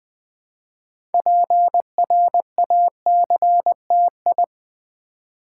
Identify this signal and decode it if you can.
Morse code, heard as a keyed tone, about 20 wpm, 700 Hz. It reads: PRACTI
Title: Build A Word - 3 Letters At A Time - 8-Letter Words - 20wpm